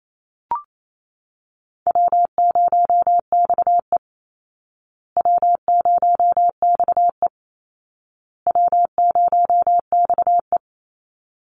Morse code, heard as a keyed tone, about 28 wpm, 700 Hz. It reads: W0XE W0XE W0XE
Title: Easy Call Signs Repeated 3x - 28wpm